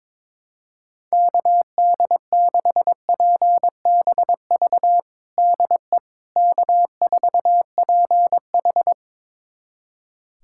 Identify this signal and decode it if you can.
Morse code, heard as a keyed tone, about 21 wpm, 700 Hz. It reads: KD6PBV DE K4PH
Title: USA Callsign DE Callsign 22wpm